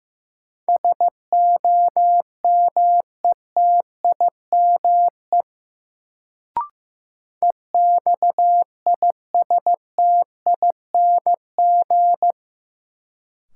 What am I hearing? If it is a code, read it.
Morse code, heard as a keyed tone, about 15 wpm, 700 Hz. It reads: SOMETIME EXISTING
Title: Sentences from Top 1500 Words - New Word Drill Alt - 15wpm